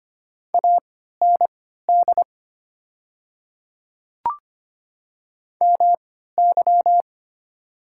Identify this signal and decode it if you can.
Morse code, heard as a keyed tone, about 25 wpm, spaced as for 14 wpm, 700 Hz. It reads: AND MY
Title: 3x Character Spacing - Top 100 Words - 25wpm